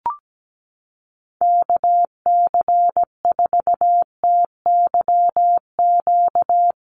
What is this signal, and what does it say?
Morse code, heard as a keyed tone, about 17 wpm, 700 Hz. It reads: KC4TYQ
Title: USA Call Signs Repeated 3x 17wpm